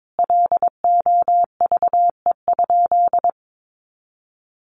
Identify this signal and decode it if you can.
Morse code, heard as a keyed tone, about 22 wpm, 700 Hz. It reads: LOVE?
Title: Sentences from Top 1500 Words 22wpm